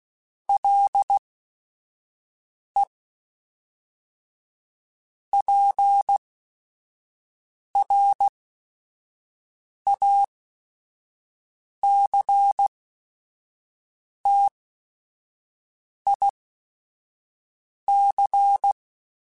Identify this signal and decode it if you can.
Morse code, heard as a keyed tone, about 16 wpm, spaced as for 5 wpm, 800 Hz. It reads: LE PRACTIC